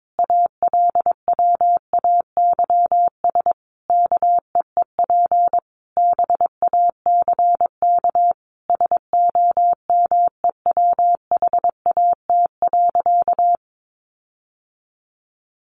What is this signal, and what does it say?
Morse code, heard as a keyed tone, about 22 wpm, 700 Hz. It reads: ALWAYS KEEP BACK SOMEWHAT.
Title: Sentences from Top 1300 Words 22wpm